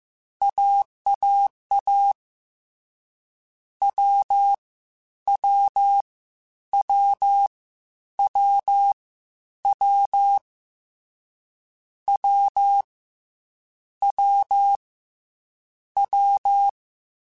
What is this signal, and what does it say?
Morse code, heard as a keyed tone, about 15 wpm, 800 Hz. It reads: AAA W W W W W W W W
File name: Difficult Combos / Difficult Combos-mp3s / AWJ ShortRun @13